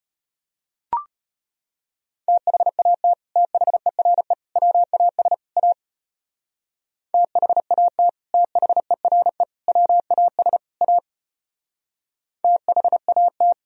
Morse code, heard as a keyed tone, about 31 wpm, 700 Hz. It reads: THAT THERE WAS A THAT THERE WAS A THAT
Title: Speed-Racing - Sets of 4 Words 25wpm